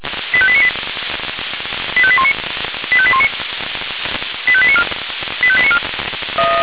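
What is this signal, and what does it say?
Five-tone selective call (ZVEI), heard as quick runs of five tones, tones of about 70 ms. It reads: 85898 85819 85819 85894 85894
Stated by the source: ZVEI five-tone selective call 85894, received by SDR.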